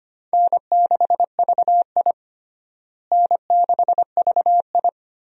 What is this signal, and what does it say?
Morse code, heard as a keyed tone, about 25 wpm, 700 Hz. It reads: N6VI N6VI
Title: USA Call Signs - 4 Characters - Repeated 2x - 25wpm